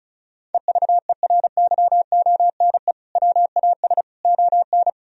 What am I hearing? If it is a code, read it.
Morse code, heard as a keyed tone, about 36 wpm, 700 Hz. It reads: EVERYONE WAS ON
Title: Sentences from Top 1000 Words - Review Part 1 - 35wpm